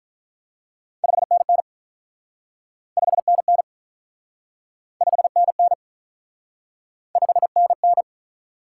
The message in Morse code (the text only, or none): 5NN 5NN 5NN 5NN